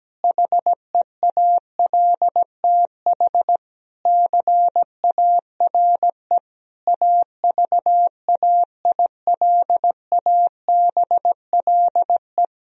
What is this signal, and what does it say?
Morse code, heard as a keyed tone, about 17 wpm, 700 Hz. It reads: HEALTH CARE AVAILABLE
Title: Sentences from Top 1000 Words - Review Part 3 - 17wpm